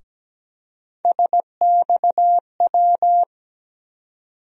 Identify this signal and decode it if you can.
Morse code, heard as a keyed tone, about 17 wpm, 700 Hz. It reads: SXW
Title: USA Call Sign Postfixes - 3 Letters 17wpm